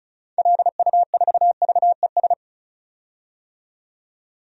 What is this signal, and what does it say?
Morse code, heard as a keyed tone, about 36 wpm, 700 Hz. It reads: LU4VES
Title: DXCC-100 Call Signs - 6 Characters - 35wpm